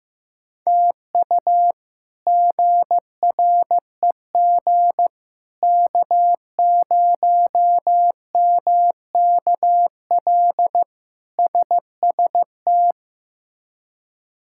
Morse code, heard as a keyed tone, about 15 wpm, 700 Hz. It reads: TU GREG K0MKL SST
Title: K1USN SST QSO Practice - 15wpm